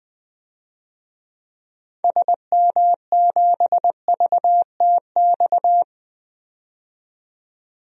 Morse code, heard as a keyed tone, about 20 wpm, 700 Hz. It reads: SM7VTX
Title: DXCC-100 Call Signs - 6 Characters - 20wpm